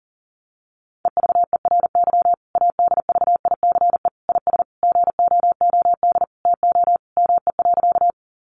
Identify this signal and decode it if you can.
Morse code, heard as a keyed tone, about 40 wpm, 700 Hz. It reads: EVERY ADVICE IS GOOD TO ME.
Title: Sentences from Top 1100 Words 40wpm